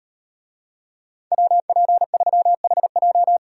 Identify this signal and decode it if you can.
Morse code, heard as a keyed tone, about 36 wpm, 700 Hz. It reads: WP3HJ